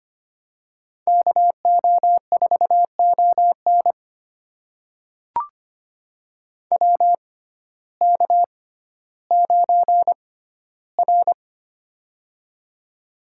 Morse code, heard as a keyed tone, about 25 wpm, spaced as for 14 wpm, 700 Hz. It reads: KO4ON WK9R
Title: USA Call Signs - 6x Character Spacing - 25wpm